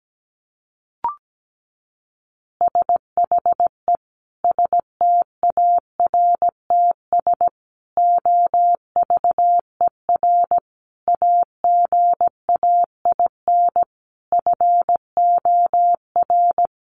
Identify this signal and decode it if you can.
Morse code, heard as a keyed tone, about 17 wpm, 700 Hz. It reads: SHE STARTS OVER AGAIN FOR
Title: Easy Phrases 17wpm